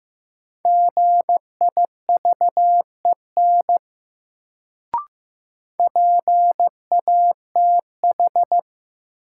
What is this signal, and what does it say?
Morse code, heard as a keyed tone, about 15 wpm, 700 Hz. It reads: GIVEN PATH
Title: Sentences from Top 1400 Words - New Word Drill - 15wpm